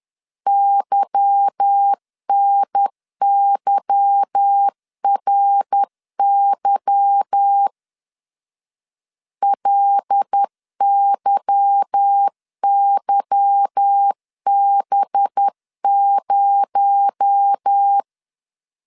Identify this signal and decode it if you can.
Morse code, heard as a keyed tone, about 10 wpm, 800 Hz. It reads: YNYRY LYYB0
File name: LEZIONI / 8 YJ45 / 8 Y 10wpm 2615